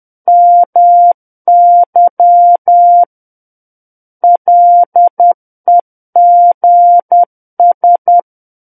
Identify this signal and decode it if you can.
Morse code, heard as a keyed tone, about 10 wpm, 700 Hz. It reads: MY LEGS
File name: Book One / 01-Chapter 1 - 10 WPM